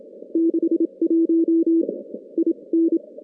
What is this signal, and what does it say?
Morse code, heard as a keyed tone, about 27 wpm, spaced as for 23 wpm, 330 Hz. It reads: 61 IN
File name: contests / CW-SS-k5tr- -2100